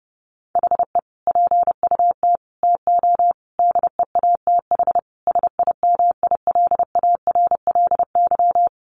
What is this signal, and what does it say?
Morse code, heard as a keyed tone, about 28 wpm, 700 Hz. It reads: HE PUT TO DEATH SIMILARLY
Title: Sentences from Top 1500 Words 30wpm